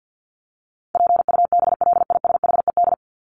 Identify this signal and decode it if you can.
Morse code, heard as a keyed tone, about 50 wpm, 700 Hz. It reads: PUBLISHED
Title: ED Suffix - Top 100 Words with Base Words 50wpm